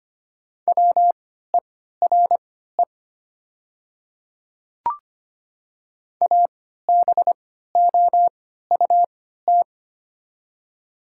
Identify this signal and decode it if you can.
Morse code, heard as a keyed tone, about 25 wpm, spaced as for 14 wpm, 700 Hz. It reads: WERE ABOUT
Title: 3x Character Spacing - Top 200 Words - 25wpm